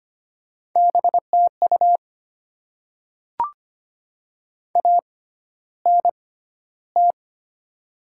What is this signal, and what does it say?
Morse code, heard as a keyed tone, about 25 wpm, spaced as for 13 wpm, 700 Hz. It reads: BTU ANT
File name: Top 20 QSO Elements - 6x Character Spacing - 25wpm